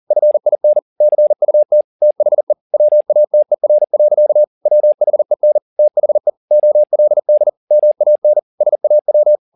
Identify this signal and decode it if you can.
Morse code, heard as a keyed tone, about 40 wpm, 600 Hz. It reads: FIN CUT THE WATER. WHEN THE OLD MAN SAW
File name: the old man and the sea 2 40wpm